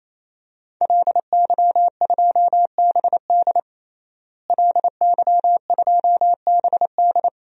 Morse code, heard as a keyed tone, about 27 wpm, 700 Hz. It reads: LY2BD LY2BD
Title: DXCC-100 Call Signs - 5 Characters - Repeated 3x - 28wpm